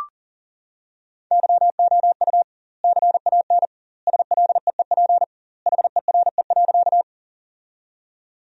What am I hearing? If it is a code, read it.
Morse code, heard as a keyed tone, about 40 wpm, 700 Hz. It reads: YOU CAN SLEEP HERE.
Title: 2x Word-Spacing - Sentences from Top 500 Words - 40wpm